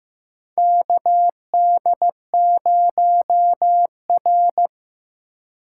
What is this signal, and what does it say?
Morse code, heard as a keyed tone, about 15 wpm, 700 Hz. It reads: KD0R
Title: USA Call Signs - 4 Characters - 2x Character Spacing - 15wpm